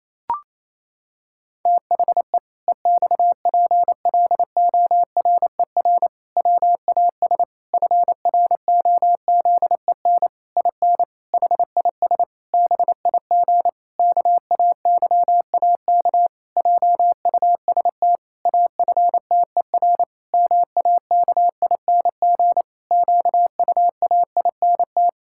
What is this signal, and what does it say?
Morse code, heard as a keyed tone, about 28 wpm, 700 Hz. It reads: THE EXPLORER WAS FROZEN IN HIS BIG KAYAK JUST AFTER MAKING QUAINT
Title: Pangram Challenge v2 - 28wpm